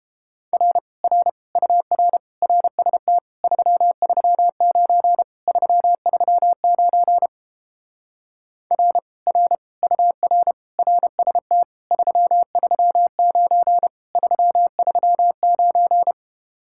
Morse code, heard as a keyed tone, about 31 wpm, 700 Hz. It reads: R R UR RST 339 339 R R UR RST 339 339